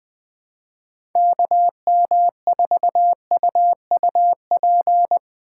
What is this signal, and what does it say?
Morse code, heard as a keyed tone, about 20 wpm, 700 Hz. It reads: KM4UUP
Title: USA Call Signs - 3x Character Spacing - 20wpm